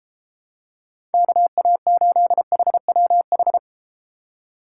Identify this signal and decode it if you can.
Morse code, heard as a keyed tone, about 32 wpm, 700 Hz. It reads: KA8HWH